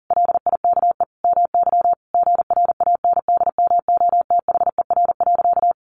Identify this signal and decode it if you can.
Morse code, heard as a keyed tone, about 40 wpm, 700 Hz. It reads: LIKE MY GRANDMOTHER.
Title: Sentences from Top 1200 Words 40wpm